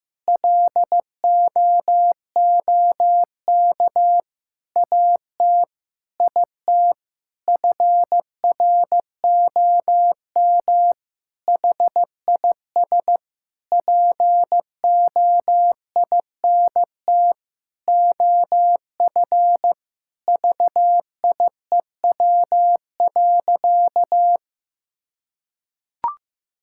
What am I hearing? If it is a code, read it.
Morse code, heard as a keyed tone, about 15 wpm, 700 Hz. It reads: LOOK AT IT FROM HIS POINT OF VIEW.
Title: Sentences from Top 1000 Words - Review Part 6 - 15wpm